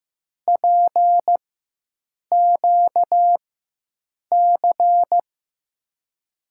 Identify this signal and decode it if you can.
Morse code, heard as a keyed tone, about 15 wpm, spaced as for 7 wpm, 700 Hz. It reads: PQC
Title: USA Call Signs - 4x Character Spacing - 15wpm